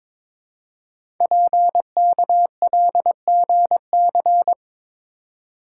Morse code, heard as a keyed tone, about 22 wpm, 700 Hz. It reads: PKLGC
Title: Code Groups - 5 Letter:Numbers - 22wpm